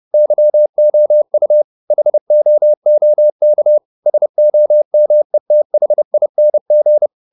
Morse code, heard as a keyed tone, about 30 wpm, 600 Hz. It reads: YOU HOOK SOMETHING
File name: the old man and the sea / the old man and the sea 1 30wpm